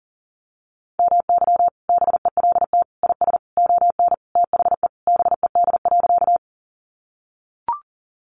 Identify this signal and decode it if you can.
Morse code, heard as a keyed tone, about 40 wpm, 700 Hz. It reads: MY BELT IS ON THE BED.
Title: Sentences from Top 1400 Words 40wpm